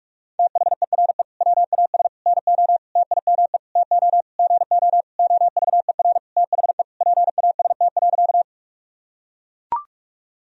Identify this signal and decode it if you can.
Morse code, heard as a keyed tone, about 43 wpm, 700 Hz. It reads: THERE WAS NO TIME TO GO OVER THE PAST.
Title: Sentences from Top 1000 Words - Review Part 3 - 45wpm